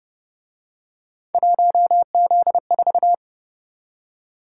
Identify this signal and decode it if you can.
Morse code, heard as a keyed tone, about 29 wpm, 700 Hz. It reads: 1Z4
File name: Speed-Racing - Three Letter-Number Combinations - 25wpm